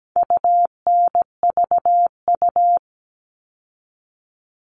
Morse code, heard as a keyed tone, about 17 wpm, 700 Hz. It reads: UNVU
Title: All Cut Numbers - 4-Digit 17wpm